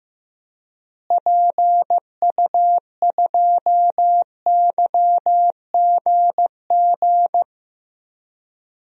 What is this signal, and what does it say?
Morse code, heard as a keyed tone, about 15 wpm, 700 Hz. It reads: PU2YGG
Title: DXCC-100 Call Signs - 6 Characters - Repeated 3x - 15wpm